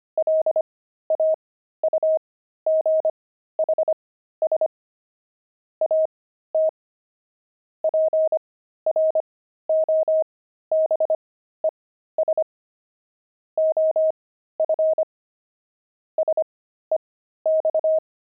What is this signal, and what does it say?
Morse code, heard as a keyed tone, about 25 wpm, spaced as for 13 wpm, 630 Hz. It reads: LAUGHS AT PROBES OF SEX
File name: pangram-fs-13-cs-25-wived